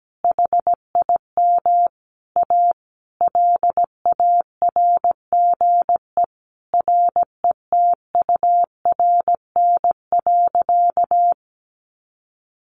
Morse code, heard as a keyed tone, about 17 wpm, 700 Hz. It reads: HIM A LARGE RETURN.
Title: Sentences from Top 1100 Words 17wpm